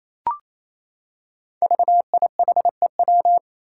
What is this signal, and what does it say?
Morse code, heard as a keyed tone, about 29 wpm, 700 Hz. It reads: VIHEW
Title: Code Groups - 5 Letter:Numbers - 28wpm